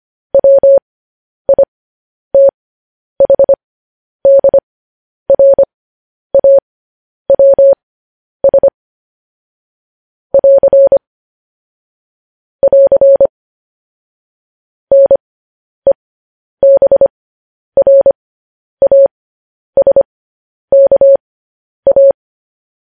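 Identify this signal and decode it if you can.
Morse code, heard as a keyed tone, about 25 wpm, spaced as for 10 wpm, 550 Hz. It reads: WITHDRAWS <AR> <AR> NEBRASKA
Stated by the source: Morse code news headlines.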